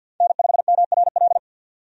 Morse code, heard as a keyed tone, about 50 wpm, 700 Hz. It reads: N5MRL
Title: USA Call Signs - 5 Characters - 50wpm